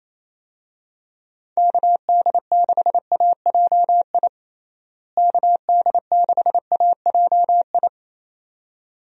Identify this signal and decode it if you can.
Morse code, heard as a keyed tone, about 28 wpm, 700 Hz. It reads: KD6AJI KD6AJI
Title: USA Call Signs - 6 Characters - Repeated 2x - 28wpm